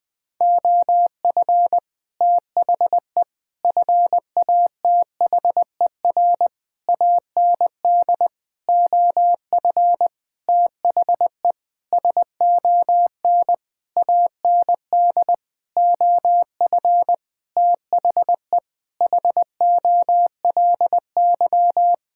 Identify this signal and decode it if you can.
Morse code, heard as a keyed tone, about 20 wpm, 700 Hz. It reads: OF THE FATHER AND OF THE SON AND OF THE HOLY
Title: Sentences with 9-Word Phrases - 20wpm